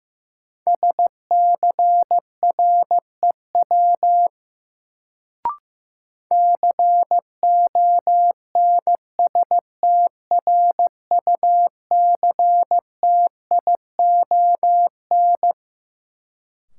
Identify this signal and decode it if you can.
Morse code, heard as a keyed tone, about 15 wpm, 700 Hz. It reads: SCREW CONSTRUCTION
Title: Sentences from Top 1600 Words - New Word Drill - 15wpm